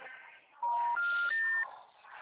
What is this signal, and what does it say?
Special information tone, with noise.
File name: out- -2005